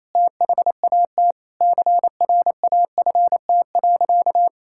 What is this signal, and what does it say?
Morse code, heard as a keyed tone, about 28 wpm, 700 Hz. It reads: THAT CRAFT.
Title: Sentences from Top 1000 Words - Review Part 6 - 28wpm